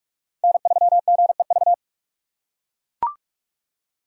Morse code, heard as a keyed tone, about 43 wpm, 700 Hz. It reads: N3GEV
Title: Code Groups - 5 Letter:Numbers - 45wpm